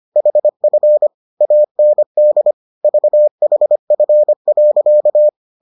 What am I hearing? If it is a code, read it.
Morse code, heard as a keyed tone, about 25 wpm, 600 Hz. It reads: HF AND VHF.